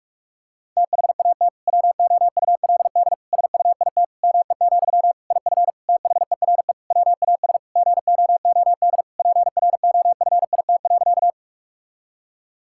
Morse code, heard as a keyed tone, about 42 wpm, 700 Hz. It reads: THAT WOULD SUIT ME, IF THERE WAS GOOD PROFIT.
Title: Sentences from Top 1600 Words 45wpm